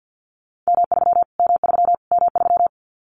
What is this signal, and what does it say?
Morse code, heard as a keyed tone, about 50 wpm, 700 Hz. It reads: M3 M3 M3
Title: Callsign Prefixes for Top 25 DXCC - Repeated 3x - 50wpm